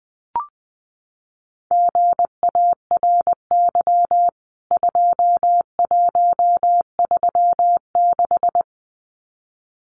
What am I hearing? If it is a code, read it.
Morse code, heard as a keyed tone, about 20 wpm, 700 Hz. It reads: GARY 2136
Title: CWT Exchange 20wpm